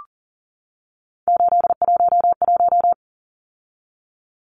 Morse code, heard as a keyed tone, about 39 wpm, 700 Hz. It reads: 811
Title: Numbers - 3-Digit 40wpm